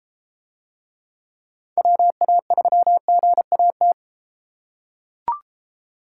Morse code, heard as a keyed tone, about 34 wpm, 700 Hz. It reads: WA3GAT